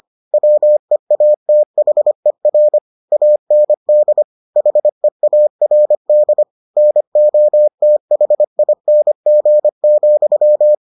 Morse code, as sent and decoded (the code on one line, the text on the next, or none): .-- . .- - .... .-. / .- -. -.. / .... .- .-. -.. / -. --- - .... .. -. --. --..--
WEATHER AND HEARD NOTHING,